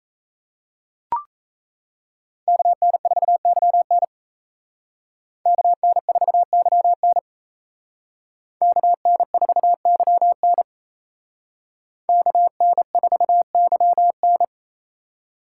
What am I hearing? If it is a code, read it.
Morse code, heard as a keyed tone, about 35 wpm, 700 Hz. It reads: KN4YN KN4YN KN4YN KN4YN
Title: Speed Racing - USA Call Signs 28wpm